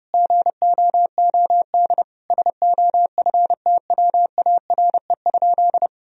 Morse code, heard as a keyed tone, about 30 wpm, 700 Hz. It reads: GOOD SOFTWARE?
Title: Sentences from Top 1000 Words - Review Part 3 - 30wpm